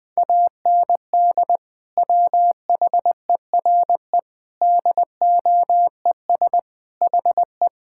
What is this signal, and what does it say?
Morse code, heard as a keyed tone, about 21 wpm, 700 Hz. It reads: AND WHERE DOES HE
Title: Sentences from Top 1000 Words - Review Part 5 - 20wpm